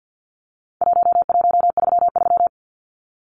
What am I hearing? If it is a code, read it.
Morse code, heard as a keyed tone, about 50 wpm, 700 Hz. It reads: 1133
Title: Numbers - 4-Digit 50wpm